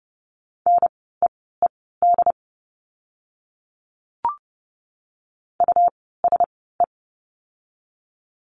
Morse code, heard as a keyed tone, about 30 wpm, spaced as for 17 wpm, 700 Hz. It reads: NEED USE
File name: Lesson 17f - Letter u - Words 30wpm